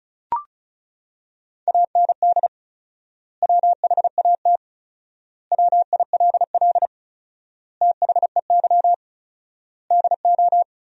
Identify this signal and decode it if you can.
Morse code, heard as a keyed tone, about 34 wpm, 700 Hz. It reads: AND WHAT WILL THEY DO
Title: 4x Word-Spacing - Sentences from Top 200 Words - 35wpm